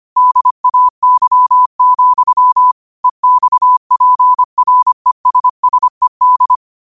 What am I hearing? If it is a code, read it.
Morse code, heard as a keyed tone, about 25 wpm, 1000 Hz. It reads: DAY, EXPRESSED
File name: 25wpm